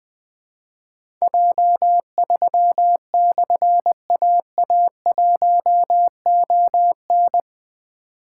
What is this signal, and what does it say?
Morse code, heard as a keyed tone, about 20 wpm, 700 Hz. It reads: J3/AA1ON
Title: Difficult Call Signs 20wpm